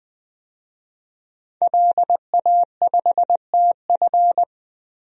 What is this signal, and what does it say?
Morse code, heard as a keyed tone, about 20 wpm, 700 Hz. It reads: LA5TF
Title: DXCC-100 Call Signs - 5 Characters - Repeated 2x - 20wpm